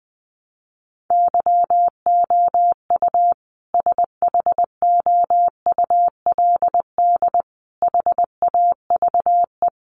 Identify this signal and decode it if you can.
Morse code, heard as a keyed tone, about 20 wpm, 700 Hz. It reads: YOU SHOULD HAVE
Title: Sentences from Top 1300 Words 20wpm